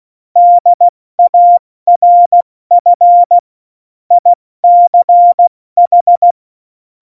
Morse code, heard as a keyed tone, about 16 wpm, 700 Hz. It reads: DARF ICH
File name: Morsekurs / Witze / Witz 16. April 2019, 70 F80